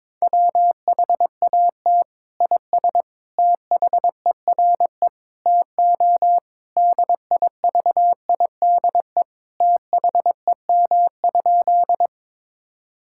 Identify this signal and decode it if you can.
Morse code, heard as a keyed tone, about 22 wpm, 700 Hz. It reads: WHAT IS THERE TO DIVIDE THEM?
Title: Sentences from Top 1600 Words 22wpm